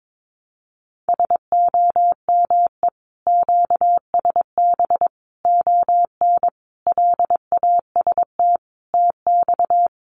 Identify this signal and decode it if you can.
Morse code, heard as a keyed tone, about 22 wpm, 700 Hz. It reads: SOME QSB ON LAST TX
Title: Common QSO Phrases 22wpm